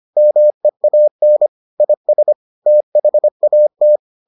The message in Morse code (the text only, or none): MEAN IS THAT